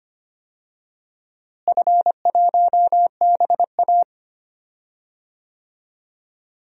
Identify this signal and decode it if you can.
Morse code, heard as a keyed tone, about 25 wpm, 700 Hz. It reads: F1BA
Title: DXCC-100 Call Signs - 4 Characters - Repeated 2x - 25wpm